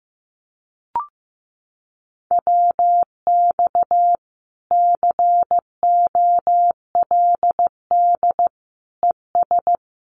Morse code, heard as a keyed tone, about 14 wpm, 700 Hz. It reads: WX COLD ES
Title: Common QSO Phrases Encore 15wpm